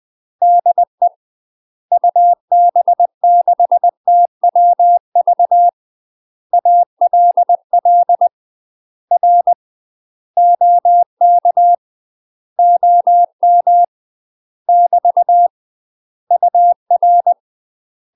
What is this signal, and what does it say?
Morse code, heard as a keyed tone, about 20 wpm, 700 Hz. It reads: DE UB6TWV ALL R OK OM <BT> UR
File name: UK MASTER QSOs TRAINING / 20WPM / UK MASTER QSOs 0008